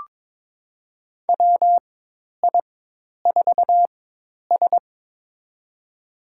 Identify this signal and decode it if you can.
Morse code, heard as a keyed tone, about 22 wpm, spaced as for 10 wpm, 700 Hz. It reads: WI4S
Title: USA Call Signs - 4 Characters - 4x Character Spacing - 22wpm